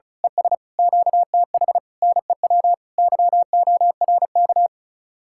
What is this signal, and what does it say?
Morse code, heard as a keyed tone, about 32 wpm, 700 Hz. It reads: ES QTH NEW YORK
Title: Speed-Racing Common QSO Phrases Encore - 35wpm